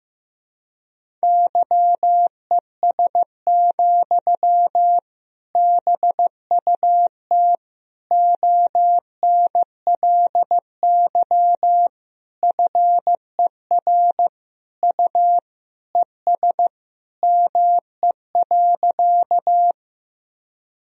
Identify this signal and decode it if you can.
Morse code, heard as a keyed tone, about 16 wpm, 700 Hz. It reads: YES, BUT ONLY FER U ES ME.
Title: Sentences from Top 500 Words and Abbreviations 15wpm